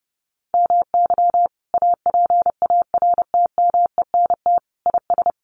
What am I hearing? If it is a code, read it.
Morse code, heard as a keyed tone, about 28 wpm, 700 Hz. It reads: MY APARTMENT IS
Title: Sentences from Top 900 Words 30wpm